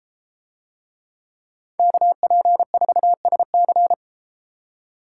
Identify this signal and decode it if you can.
Morse code, heard as a keyed tone, about 31 wpm, 700 Hz. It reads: KP4SC